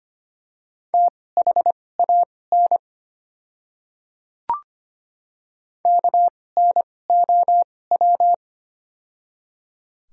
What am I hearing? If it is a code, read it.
Morse code, heard as a keyed tone, about 25 wpm, spaced as for 18 wpm, 700 Hz. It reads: THAN KNOW
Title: 2x Character Spacing - Top 100 Words - 25wpm